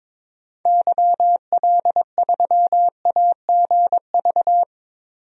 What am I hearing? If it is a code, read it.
Morse code, heard as a keyed tone, about 22 wpm, 700 Hz. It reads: YL3AGV
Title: DXCC-100 Call Signs - 6 Characters - Repeated 2x - 22wpm